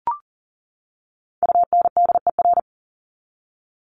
Morse code, heard as a keyed tone, about 38 wpm, 700 Hz. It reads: UNDER
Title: Top 200 Words 40wpm